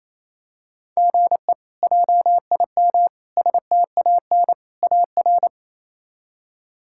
Morse code, heard as a keyed tone, about 28 wpm, 700 Hz. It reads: GE JIM STAN AR
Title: K1USN SST QSO Practice - 28wpm